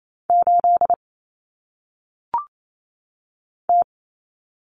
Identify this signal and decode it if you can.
Morse code, heard as a keyed tone, about 28 wpm, 700 Hz. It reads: 8 T
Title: Common Cut Numbers - 1-Digit 28wpm